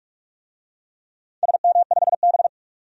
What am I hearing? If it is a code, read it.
Morse code, heard as a keyed tone, about 46 wpm, 700 Hz. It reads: SM5B
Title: DXCC-100 Call Signs - 4 Characters - 45wpm